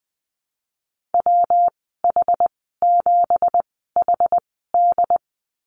Morse code, heard as a keyed tone, about 20 wpm, spaced as for 15 wpm, 700 Hz. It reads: WH7HD
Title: 173 Lesson 28g - Number 9 - Call Signs 20wpm